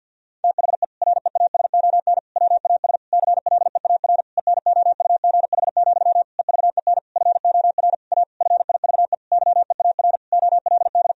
Morse code, heard as a keyed tone, about 50 wpm, 700 Hz. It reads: THE REASON WAS CLEAR ENOUGH, EVEN FOR A FIVE YEAR OLD
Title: Sentences from Top 1000 Words - Review Part 2 - 50wpm